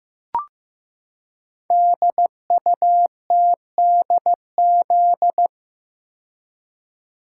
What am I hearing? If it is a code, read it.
Morse code, heard as a keyed tone, about 15 wpm, 700 Hz. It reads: DUTDZ
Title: Code Groups - 5 Letters - 15wpm